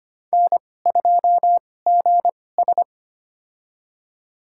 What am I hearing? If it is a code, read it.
Morse code, heard as a keyed tone, about 25 wpm, spaced as for 18 wpm, 700 Hz. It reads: N2GS
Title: USA Call Signs - 4 Characters - 2x Character Spacing - 25wpm